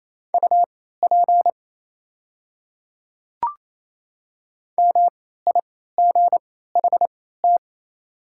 Morse code, heard as a keyed tone, about 28 wpm, spaced as for 16 wpm, 700 Hz. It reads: UP MIGHT
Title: 3x Character Spacing - Top 300 Words - 28wpm